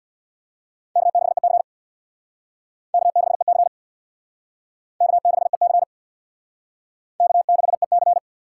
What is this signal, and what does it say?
Morse code, heard as a keyed tone, about 60 wpm, 700 Hz. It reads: K6EC K6EC K6EC K6EC
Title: Speed Racing - USA Call Signs 50wpm